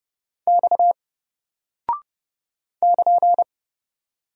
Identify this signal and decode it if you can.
Morse code, heard as a keyed tone, about 29 wpm, 700 Hz. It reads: X <KN>
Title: Single Character (Letter:Number:Punctuation:Prosign) Alt 30wpm